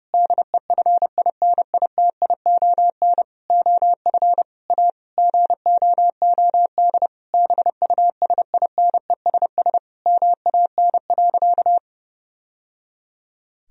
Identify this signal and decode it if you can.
Morse code, heard as a keyed tone, about 30 wpm, 700 Hz. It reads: DEFINITION OF A GOOD BUSINESS MAN.
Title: Sentences from Top 1000 Words - Review Part 4 - 30wpm